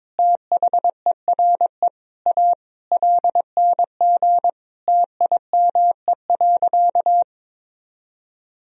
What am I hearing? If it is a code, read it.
Morse code, heard as a keyed tone, about 21 wpm, 700 Hz. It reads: THERE A LNG TIME.
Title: Sentences from Top 400 Words and Abbreviations 22wpm